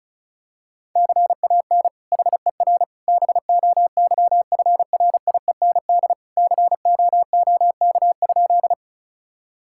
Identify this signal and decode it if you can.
Morse code, heard as a keyed tone, about 34 wpm, 700 Hz. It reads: CAN HER BOYFRIEND COOK?
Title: Sentences from Top 1000 Words - Review Part 5 - 35wpm